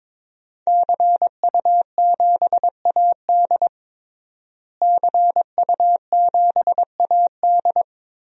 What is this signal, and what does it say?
Morse code, heard as a keyed tone, about 22 wpm, 700 Hz. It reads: CU7AD CU7AD
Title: DXCC-100 Call Signs - 5 Characters - Repeated 3x - 22wpm